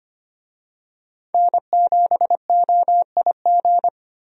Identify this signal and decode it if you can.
Morse code, heard as a keyed tone, about 25 wpm, 700 Hz. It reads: N7OIG
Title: USA Call Signs - 5 Characters - Repeated 3x - 25wpm